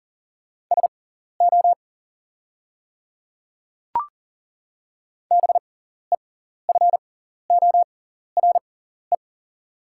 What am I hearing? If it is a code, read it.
Morse code, heard as a keyed tone, about 40 wpm, spaced as for 14 wpm, 690 Hz. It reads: SO BEFORE
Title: 6x Character Spacing - Top 300 Words - 40wpm